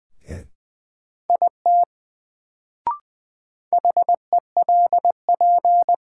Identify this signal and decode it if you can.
Morse code, heard as a keyed tone, about 21 wpm, 700 Hz. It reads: IT HELP
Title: Sets of 2 Words - New Word Drill - 20wpm